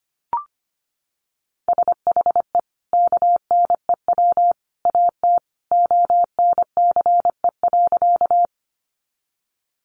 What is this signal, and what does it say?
Morse code, heard as a keyed tone, about 25 wpm, 700 Hz. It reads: SHE KNEW AT ONCE.
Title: Sentences from Top 400 Words 25wpm